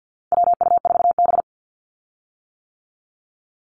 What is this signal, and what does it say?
Morse code, heard as a keyed tone, about 50 wpm, 700 Hz. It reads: WU4B
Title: Call Signs - 50wpm